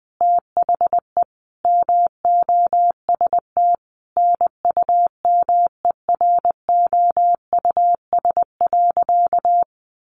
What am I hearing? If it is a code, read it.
Morse code, heard as a keyed tone, about 20 wpm, 700 Hz. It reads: THE MOST NUMEROUS.
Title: Sentences from Top 1400 Words 20wpm